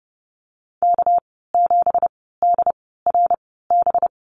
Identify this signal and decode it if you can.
Morse code, heard as a keyed tone, about 31 wpm, spaced as for 17 wpm, 700 Hz. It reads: K7DRB
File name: Lesson 37g - Letter x - Call Signs 30wpm